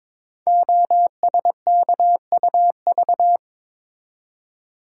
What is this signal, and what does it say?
Morse code, heard as a keyed tone, about 22 wpm, 700 Hz. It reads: OSKUV
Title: Code Groups - 5 Letters - 22wpm